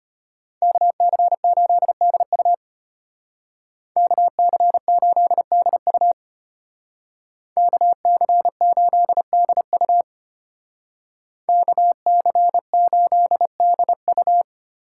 Morse code, heard as a keyed tone, about 33 wpm, 700 Hz. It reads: KC8DU KC8DU KC8DU KC8DU